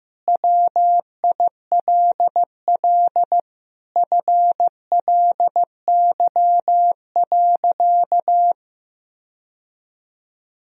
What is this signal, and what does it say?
Morse code, heard as a keyed tone, about 15 wpm, 700 Hz. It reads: WILL FLY.
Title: Sentences from Top 1000 Words - Review Part 2 - 15wpm